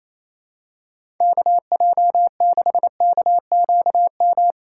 Morse code, heard as a keyed tone, about 28 wpm, 700 Hz. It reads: KJ6KQM
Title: USA Call Signs - 6 Characters - 4x Character Spacing - 28wpm